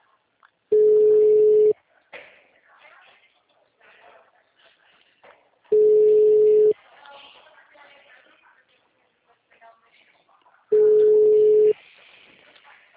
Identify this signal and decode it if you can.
Ringback tone (Europe, 425 Hz), with speech.